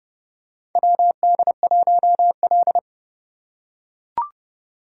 Morse code, heard as a keyed tone, about 31 wpm, 700 Hz. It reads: WD1L
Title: USA Call Signs - 4 Characters - 4x Character Spacing - 30wpm